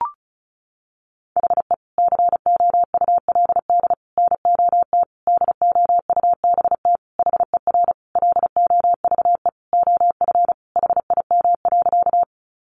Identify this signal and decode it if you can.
Morse code, heard as a keyed tone, about 34 wpm, 700 Hz. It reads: HE COULD NOT DOUBT HER LOVE OF HIM.
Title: Sentences from Top 1500 Words 35wpm